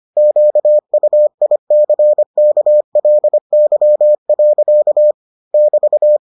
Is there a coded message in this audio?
Yes: Morse code from 0.2 to 6.2 s, heard as a keyed tone, about 25 wpm, 600 Hz. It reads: QUICKLY. <BT>